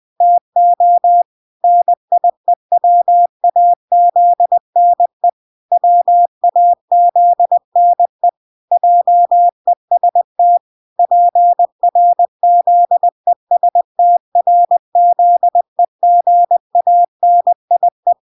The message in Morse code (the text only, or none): TO NIEWAZNE WAZNE JEST PRZESTRZEGANIE